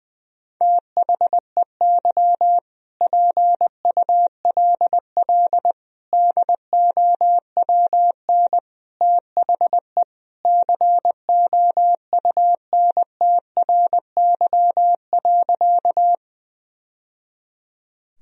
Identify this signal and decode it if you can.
Morse code, heard as a keyed tone, about 20 wpm, 700 Hz. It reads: THEY PULL DOWN THE COUNTRY.
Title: Sentences from Top 1000 Words - Review Part 2 - 20wpm